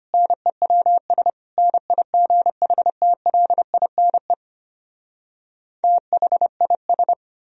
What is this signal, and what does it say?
Morse code, heard as a keyed tone, about 33 wpm, 700 Hz. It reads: NEWS NIGHTLINE THIS